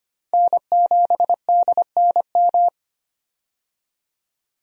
Morse code, heard as a keyed tone, about 25 wpm, 700 Hz. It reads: N7DNM
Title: USA Call Signs - 5 Characters - 25wpm